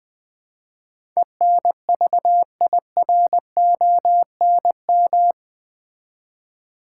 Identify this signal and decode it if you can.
Morse code, heard as a keyed tone, about 20 wpm, 700 Hz. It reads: ENVIRONM
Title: Build A Word - 4 Letters At A Time - 11-Letter Words - 20wpm